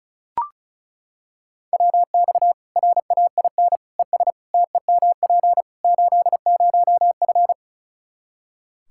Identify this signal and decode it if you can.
Morse code, heard as a keyed tone, about 34 wpm, 700 Hz. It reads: WX RAIN ES TEMP 80F
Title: K1USN SST QSO Practice 35wpm